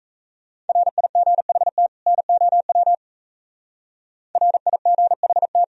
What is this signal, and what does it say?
Morse code, heard as a keyed tone, about 35 wpm, 700 Hz. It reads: RIGHT NOW RIGHT
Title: Speed-Racing - Sets of 2 Words 28wpm